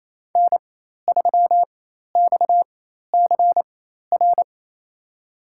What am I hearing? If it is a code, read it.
Morse code, heard as a keyed tone, about 28 wpm, spaced as for 13 wpm, 700 Hz. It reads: N3XCR
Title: USA Call Signs - 5 Characters - 4x Character Spacing - 28wpm